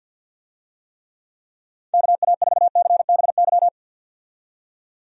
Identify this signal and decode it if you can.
Morse code, heard as a keyed tone, about 52 wpm, 700 Hz. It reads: KA4CBY
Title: USA Call Signs - 6 Characters - 50wpm